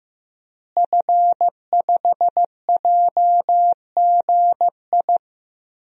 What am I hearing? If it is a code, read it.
Morse code, heard as a keyed tone, about 15 wpm, 700 Hz. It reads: F5JGI
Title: DXCC-100 Call Signs - 5 Characters - Repeated 2x - 15wpm